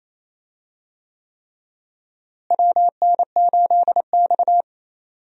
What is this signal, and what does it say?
Morse code, heard as a keyed tone, about 28 wpm, 700 Hz. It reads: WN8X
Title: USA Call Signs - 4 Characters - Repeated 3x - 28wpm